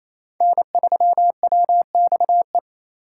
Morse code, heard as a keyed tone, about 28 wpm, 700 Hz. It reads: N3WXE
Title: USA Call Signs - 5 Characters - 4x Character Spacing - 28wpm